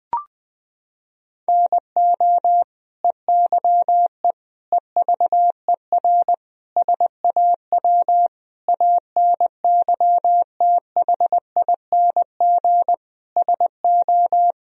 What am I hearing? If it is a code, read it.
Morse code, heard as a keyed tone, about 20 wpm, 700 Hz. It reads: NO EYE EVER SAW ANYTHING SO
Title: Sentences from Top 1000 Words - Review Part 1 - 20wpm